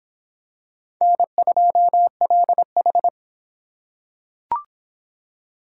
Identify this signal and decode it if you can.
Morse code, heard as a keyed tone, about 22 wpm, 700 Hz. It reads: N2LH